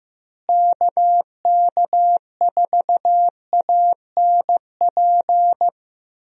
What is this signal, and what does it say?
Morse code, heard as a keyed tone, about 15 wpm, 700 Hz. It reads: KK4ANP
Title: USA Call Signs - 6 Characters - Repeated 2x - 15wpm